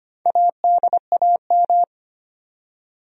Morse code, heard as a keyed tone, about 25 wpm, 700 Hz. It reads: ADAM
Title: USA Common Names - Max 5 Letters - 25wpm